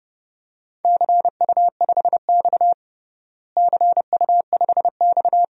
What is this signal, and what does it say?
Morse code, heard as a keyed tone, about 30 wpm, 700 Hz. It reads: CU5X CU5X
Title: DXCC-100 Call Signs - 4 Characters - Repeated 2x - 30wpm